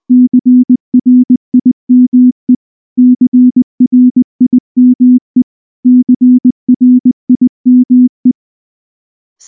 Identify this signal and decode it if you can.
Morse code, heard as a keyed tone, about 19 wpm, 250 Hz. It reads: CRIME CRIME CRIME
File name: morse / morse-training-20WPM-4